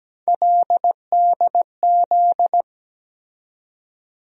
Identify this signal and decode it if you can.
Morse code, heard as a keyed tone, about 17 wpm, 700 Hz. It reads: LDZ
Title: USA Call Sign Postfixes - 3 Letters 17wpm